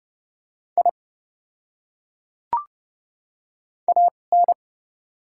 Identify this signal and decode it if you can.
Morse code, heard as a keyed tone, about 30 wpm, spaced as for 22 wpm, 700 Hz. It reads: I AN
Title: 2x Character Spacing - Top 100 Words - 30wpm